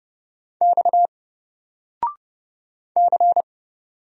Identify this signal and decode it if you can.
Morse code, heard as a keyed tone, about 31 wpm, 700 Hz. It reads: X C
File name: Single-Letter-Number-30wpm